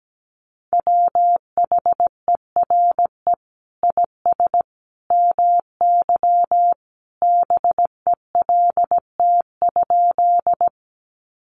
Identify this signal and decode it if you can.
Morse code, heard as a keyed tone, about 17 wpm, 700 Hz. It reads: WHERE IS MY BELT?
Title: Sentences from Top 1400 Words 17wpm